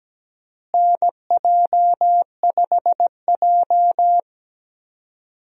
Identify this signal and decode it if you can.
Morse code, heard as a keyed tone, about 17 wpm, 700 Hz. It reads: NJ5J
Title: USA Call Signs - 4 Characters - Repeated 3x - 17wpm